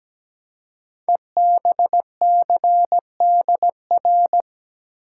Morse code, heard as a keyed tone, about 17 wpm, 700 Hz. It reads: EBCDR
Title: Code Groups - 5 Letters - 17wpm